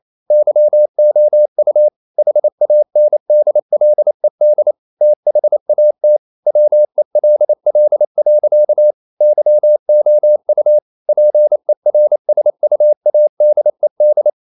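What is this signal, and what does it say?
Morse code, heard as a keyed tone, about 28 wpm, 600 Hz. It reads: YOU HANDLED THAT WELL. YOU PERSUADED
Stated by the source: Morse code practice at 28 wpm, text of Ken Follett's novel Never in English, from CW med Gnister.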